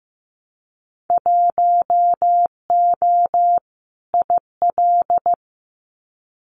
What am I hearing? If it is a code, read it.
Morse code, heard as a keyed tone, about 15 wpm, 700 Hz. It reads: 1O IL
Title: Winter Field Day Exchanges 15wpm